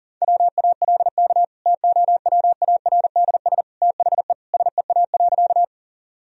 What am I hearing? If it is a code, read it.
Morse code, heard as a keyed tone, about 40 wpm, 700 Hz. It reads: WALK TOWARDS THE SEA.
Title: Sentences from Top 1000 Words - Review Part 2 - 40wpm